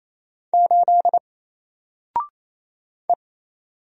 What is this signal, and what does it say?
Morse code, heard as a keyed tone, about 28 wpm, 700 Hz. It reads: 8 E
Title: Single Character (Letter:Number:Punctuation:Prosign) Alt 28wpm